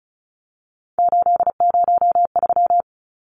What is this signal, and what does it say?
Morse code, heard as a keyed tone, about 34 wpm, 700 Hz. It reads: 803
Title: Numbers - 3-Digit 35wpm